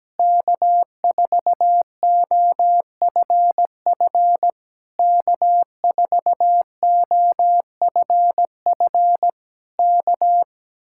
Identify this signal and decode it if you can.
Morse code, heard as a keyed tone, about 17 wpm, 700 Hz. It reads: K4OFF K4OFF K
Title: Speed-Racing Common QSO Phrases Encore - 17wpm